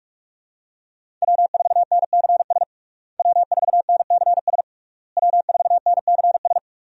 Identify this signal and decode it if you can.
Morse code, heard as a keyed tone, about 46 wpm, 700 Hz. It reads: W4NCS W4NCS W4NCS
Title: USA Call Signs - 5 Characters - Repeated 3x - 45wpm